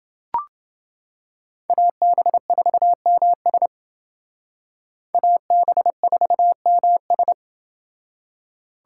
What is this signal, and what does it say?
Morse code, heard as a keyed tone, about 28 wpm, 700 Hz. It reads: AB4MS AB4MS